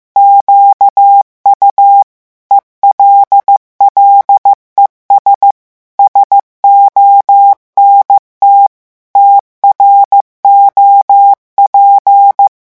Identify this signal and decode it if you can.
Morse code, heard as a keyed tone, about 15 wpm, 800 Hz. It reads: QU ELLES SONT TROP
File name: morse mp3 / revision9 citations humoristiques vitesse 14 mots minute